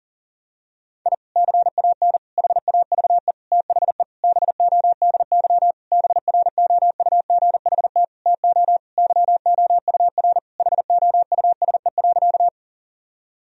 Morse code, heard as a keyed tone, about 40 wpm, 700 Hz. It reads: I CAN HAVE THE BODY BROUGHT TO YOUR HOUSE.
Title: Sentences from Top 1000 Words - Review Part 2 - 40wpm